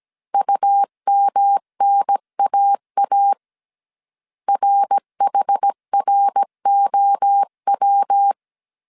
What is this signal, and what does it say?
Morse code, heard as a keyed tone, about 17 wpm, 800 Hz. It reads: UMNAA RHROW